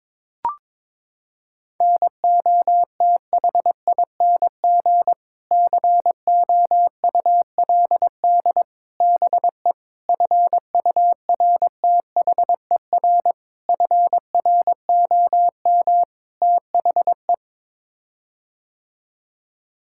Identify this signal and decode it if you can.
Morse code, heard as a keyed tone, about 22 wpm, 700 Hz. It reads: NOTHING COULD BE FURTHER FROM THE
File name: practice / 22wpm / Sets of 6 Words - 22wpm